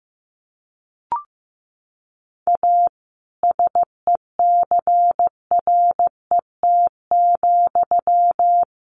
Morse code, heard as a keyed tone, about 15 wpm, 700 Hz. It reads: A SECRET,
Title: Sentences from Top 1200 Words 15wpm